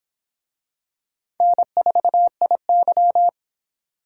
Morse code, heard as a keyed tone, about 25 wpm, 700 Hz. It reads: N4IY